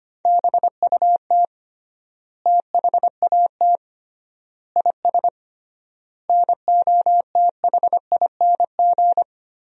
Morse code, heard as a keyed tone, about 25 wpm, 700 Hz. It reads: BUT THAT IS NOTHING
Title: 3x Word-Spacing - Sentences from Top 500 Words - 25wpm